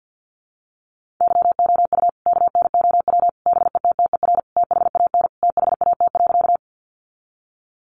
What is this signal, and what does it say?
Morse code, heard as a keyed tone, about 50 wpm, 700 Hz. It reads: YOU KNOW BETTER THAN THAT.
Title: Sentences from Top 400 Words 50wpm